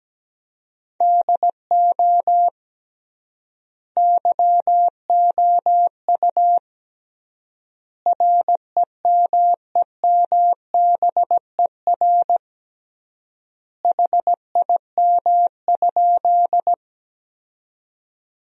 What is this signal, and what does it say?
Morse code, heard as a keyed tone, about 17 wpm, 700 Hz. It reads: DO YOU REMEMBER HIM?
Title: 3x Word-Spacing - Sentences from Top 400 Words - 17wpm